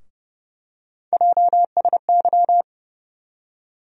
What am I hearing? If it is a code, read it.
Morse code, heard as a keyed tone, about 31 wpm, 700 Hz. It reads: JSY
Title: USA Call Sign Postfixes - 3 Letters 30wpm